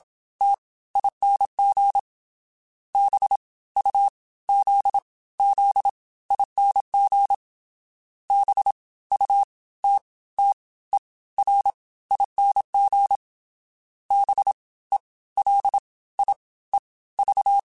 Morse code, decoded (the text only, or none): T ING B U Z Z ING B U T T E R ING B E L I E V